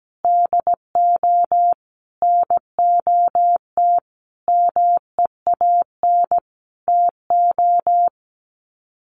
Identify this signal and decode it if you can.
Morse code, heard as a keyed tone, about 17 wpm, 700 Hz. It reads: DO NOT MEAN TO
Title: Sets of 5 Words 17wpm